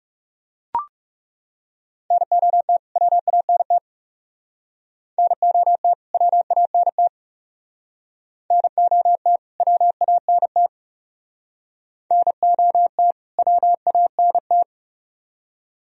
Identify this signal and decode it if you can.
Morse code, heard as a keyed tone, about 38 wpm, 700 Hz. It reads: NOT WANT NOT WANT NOT WANT NOT WANT